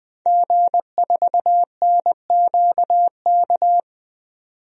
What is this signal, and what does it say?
Morse code, heard as a keyed tone, about 20 wpm, 700 Hz. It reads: G4NQK
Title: DXCC-100 Call Signs - 5 Characters - 20wpm